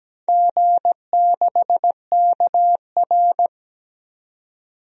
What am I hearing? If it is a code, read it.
Morse code, heard as a keyed tone, about 17 wpm, 700 Hz. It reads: G6KR
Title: DXCC-100 Call Signs - 4 Characters - Repeated 2x - 17wpm